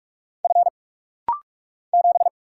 Morse code, heard as a keyed tone, about 44 wpm, 700 Hz. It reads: F 7
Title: Single Letter-Number - Rapid-Fire 45wpm